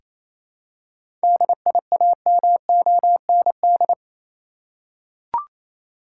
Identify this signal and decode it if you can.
Morse code, heard as a keyed tone, about 28 wpm, 700 Hz. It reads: DIAMOND
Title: Seven Letter Words - 28wpm